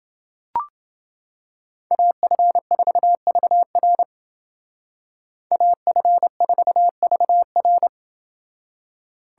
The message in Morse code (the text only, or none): AF4VR AF4VR